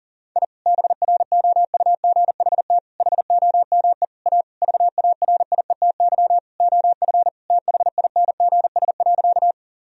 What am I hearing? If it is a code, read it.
Morse code, heard as a keyed tone, about 37 wpm, 700 Hz. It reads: I BROUGHT HOME A VARIETY OF THINGS.
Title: Sentences from Top 1000 Words - Review Part 4 - 40wpm